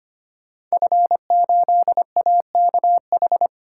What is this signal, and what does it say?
Morse code, heard as a keyed tone, about 25 wpm, 700 Hz. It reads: F8AKH
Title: Code Groups - 5 Letter:Numbers - 25wpm